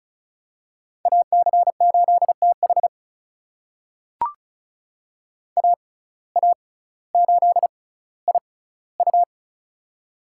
Morse code, heard as a keyed tone, about 36 wpm, spaced as for 18 wpm, 700 Hz. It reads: AC8TH AA8IU
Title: USA Call Signs - 5 Characters - 6x Character Spacing - 35wpm